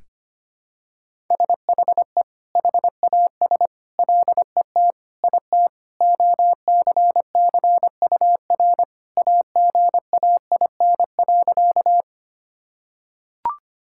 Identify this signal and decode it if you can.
Morse code, heard as a keyed tone, about 25 wpm, 700 Hz. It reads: SHE HAS LET IT OCCUR AGAIN.
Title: Sentences from Top 1000 Words - Review Part 6 - 25wpm